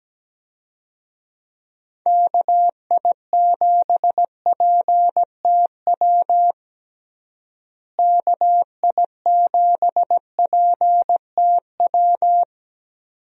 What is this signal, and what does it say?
Morse code, heard as a keyed tone, about 17 wpm, 700 Hz. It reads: KI7PTW KI7PTW
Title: USA Call Signs - 6 Characters - Repeated 2x - 17wpm